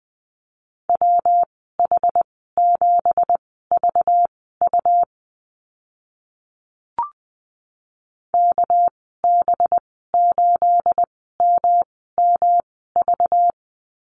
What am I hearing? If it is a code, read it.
Morse code, heard as a keyed tone, about 20 wpm, spaced as for 14 wpm, 700 Hz. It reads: WH7VU KB8MMV
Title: 231 Lesson 37g - Letter x - Call Signs 20wpm